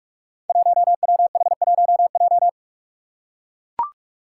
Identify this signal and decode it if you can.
Morse code, heard as a keyed tone, about 44 wpm, 700 Hz. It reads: 1WH1J
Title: Five Challenging Letter-Number Combinations 45wpm